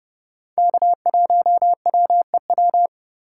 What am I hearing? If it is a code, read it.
Morse code, heard as a keyed tone, about 30 wpm, 700 Hz. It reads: K1WEW
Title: USA Call Signs - 4x Character Spacing - 30wpm